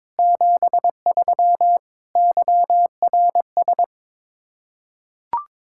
Morse code, heard as a keyed tone, about 22 wpm, 700 Hz. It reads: 73 YRS
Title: K1USN SST QSO Practice 22wpm